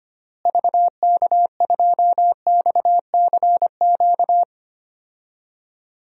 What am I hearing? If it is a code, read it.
Morse code, heard as a keyed tone, about 25 wpm, 700 Hz. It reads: VK2XCQ
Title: DXCC-100 Call Signs - 6 Characters - Repeated 3x - 25wpm